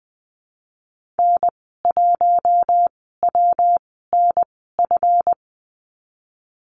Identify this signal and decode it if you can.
Morse code, heard as a keyed tone, about 19 wpm, spaced as for 14 wpm, 700 Hz. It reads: N1WNF
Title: 140 Lesson 23g - Letter f - Call Signs 20wpm